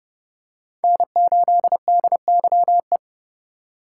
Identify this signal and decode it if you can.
Morse code, heard as a keyed tone, about 30 wpm, 700 Hz. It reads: N8DYE
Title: USA Call Signs - 5 Characters - 3x Character Spacing - 30wpm